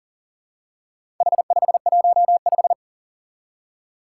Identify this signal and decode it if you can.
Morse code, heard as a keyed tone, about 42 wpm, 700 Hz. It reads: H515
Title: Four Challenging Letter-Number Combinations 40wpm